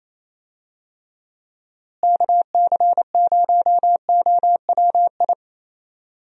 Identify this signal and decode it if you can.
Morse code, heard as a keyed tone, about 28 wpm, 700 Hz. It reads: KC0OWI
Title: USA Call Signs - 6 Characters - Repeated 3x - 28wpm